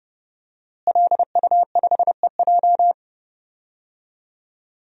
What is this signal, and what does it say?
Morse code, heard as a keyed tone, about 31 wpm, 700 Hz. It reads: LU5EJ
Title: DXCC-100 Call Signs - 5 Characters - Repeated 2x - 30wpm